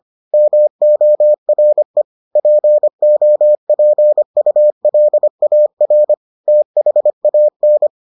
Morse code, heard as a keyed tone, about 25 wpm, 600 Hz. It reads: MORE POPULAR THAN